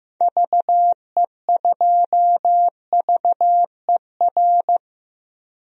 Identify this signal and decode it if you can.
Morse code, heard as a keyed tone, about 16 wpm, 700 Hz. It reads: VE2VER
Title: DXCC-100 Call Signs - 6 Characters - 15wpm